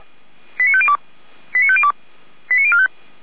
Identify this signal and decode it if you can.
Five-tone selective call (ZVEI), heard as quick runs of five tones, tones of about 70 ms. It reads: 89502 89502 89045
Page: SDR/ZVEI